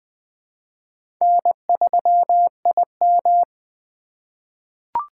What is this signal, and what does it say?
Morse code, heard as a keyed tone, about 20 wpm, 700 Hz. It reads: N3IM
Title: USA Call Signs - 4 Characters - 3x Character Spacing - 20wpm